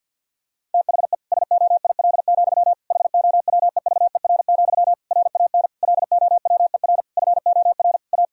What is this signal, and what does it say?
Morse code, heard as a keyed tone, about 52 wpm, 700 Hz. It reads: THE SOIL, HOWEVER, RAN LOWER FOR A
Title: Sentences from Top 1000 Words - Review Part 4 - 50wpm